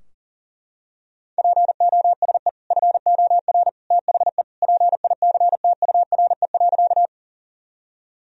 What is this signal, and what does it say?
Morse code, heard as a keyed tone, about 40 wpm, 700 Hz. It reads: POSE FOR THE PICTURE.
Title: Sentences from Top 1000 Words - Review Part 2 - 40wpm